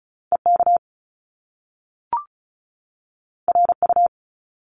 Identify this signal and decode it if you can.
Morse code, heard as a keyed tone, about 36 wpm, 700 Hz. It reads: EK RU
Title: Top Two Letter Combinations in Words 35wpm